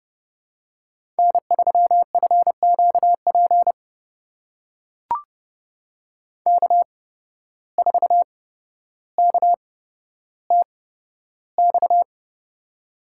Morse code, heard as a keyed tone, about 30 wpm, spaced as for 13 wpm, 700 Hz. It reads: N3FQP K4KTX
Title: USA Call Signs - 5 Characters - 8x Character Spacing - 30wpm